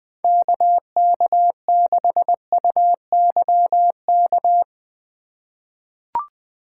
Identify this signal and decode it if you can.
Morse code, heard as a keyed tone, about 20 wpm, 700 Hz. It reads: KK6UYK
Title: USA Call Signs - 6 Characters - 2x Character Spacing - 20wpm